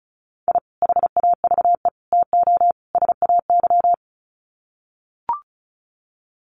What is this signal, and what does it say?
Morse code, heard as a keyed tone, about 34 wpm, 700 Hz. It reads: I HAVE TO SAY
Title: Sets of 4 Words 35wpm